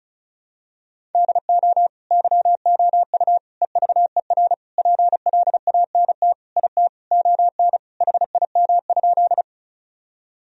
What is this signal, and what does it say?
Morse code, heard as a keyed tone, about 36 wpm, 700 Hz. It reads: DO YOU EVER PLANT IT ON HIM?
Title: Sentences from Top 1000 Words - Review Part 1 - 35wpm